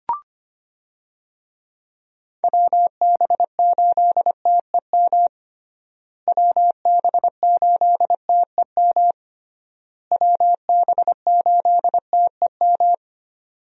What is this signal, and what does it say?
Morse code, heard as a keyed tone, about 25 wpm, 700 Hz. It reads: WB8TEM WB8TEM WB8TEM
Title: USA Call Signs - 6 Characters - Repeated 3x - 25wpm